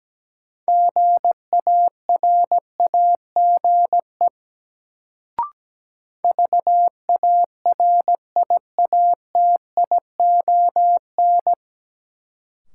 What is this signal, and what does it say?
Morse code, heard as a keyed tone, about 17 wpm, 700 Hz. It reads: GARAGE VARIATION
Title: Sentences from Top 1600 Words - New Word Drill - 17wpm